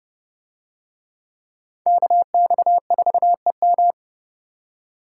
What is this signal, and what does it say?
Morse code, heard as a keyed tone, about 28 wpm, 700 Hz. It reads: KX4EM